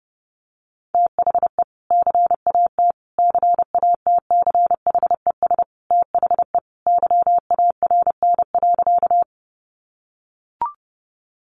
Morse code, heard as a keyed tone, about 30 wpm, 700 Hz. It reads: THE CAT CATCHES THE YARN.
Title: Additional Easy Phrases 30wpm